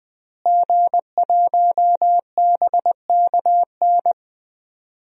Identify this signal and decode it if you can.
Morse code, heard as a keyed tone, about 20 wpm, 700 Hz. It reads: G1BKN
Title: DXCC-100 Call Signs - 5 Characters - Repeated 3x - 20wpm